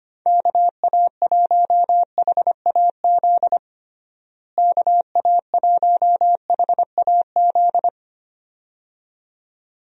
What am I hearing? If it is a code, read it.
Morse code, heard as a keyed tone, about 25 wpm, 700 Hz. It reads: KA1HAZ KA1HAZ
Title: USA Call Signs - 6 Characters - Repeated 2x - 25wpm